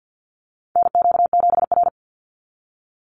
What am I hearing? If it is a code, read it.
Morse code, heard as a keyed tone, about 49 wpm, 700 Hz. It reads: NQ7R
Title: NA SOTA Chaser Callsigns 50wpm